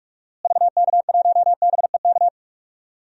Morse code, heard as a keyed tone, about 42 wpm, 700 Hz. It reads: VK1BEK
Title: DXCC-100 Call Signs - 6 Characters - Repeated 3x - 45wpm